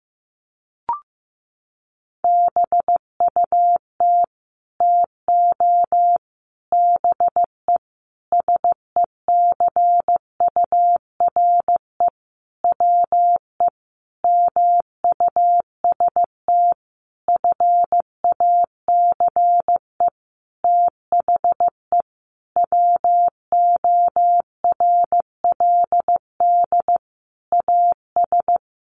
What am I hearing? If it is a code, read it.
Morse code, heard as a keyed tone, about 15 wpm, 700 Hz. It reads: BUT TO BE SECURE WE MUST FACE THE WORLD AS
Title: Sentences from Top 1200 Words 15wpm